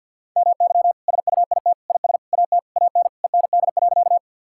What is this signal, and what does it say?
Morse code, heard as a keyed tone, about 50 wpm, 700 Hz. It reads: MY SUIT IS AT AN END.
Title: Sentences from Top 1600 Words 50wpm